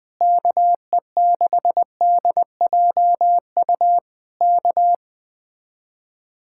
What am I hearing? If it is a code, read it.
Morse code, heard as a keyed tone, about 20 wpm, 700 Hz. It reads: KE6DJU K
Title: Speed-Racing Common QSO Phrases Encore - 17wpm